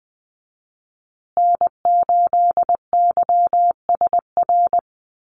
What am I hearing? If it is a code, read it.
Morse code, heard as a keyed tone, about 20 wpm, 700 Hz. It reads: N8YSR
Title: USA Call Signs Repeated 3x 20wpm